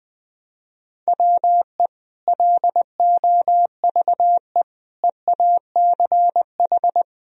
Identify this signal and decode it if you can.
Morse code, heard as a keyed tone, about 20 wpm, 700 Hz. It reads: WE LOVE EACH
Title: Sentences from Top 1000 Words - Review Part 1 - 20wpm